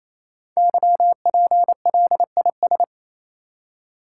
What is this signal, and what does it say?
Morse code, heard as a keyed tone, about 29 wpm, 700 Hz. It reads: YPLIS
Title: Code Groups - 5 Letters - 28wpm